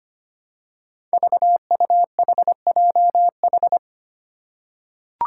Morse code, heard as a keyed tone, about 25 wpm, 700 Hz. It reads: VUHJH
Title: Five Challenging Letter-Number Combinations 25wpm